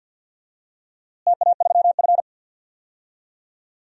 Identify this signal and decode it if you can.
Morse code, heard as a keyed tone, about 47 wpm, 700 Hz. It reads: TA3F
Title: DXCC-100 Call Signs - 4 Characters - Repeated 2x - 50wpm